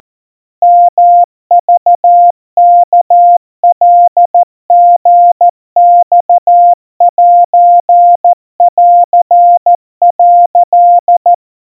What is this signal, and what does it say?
Morse code (Japanese Wabun code), heard as a keyed tone, about 14 wpm, 700 Hz. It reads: ヨクワカリマセン」
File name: kawa / Morse sample / kamyu 50 cpm 700 Hz Crispy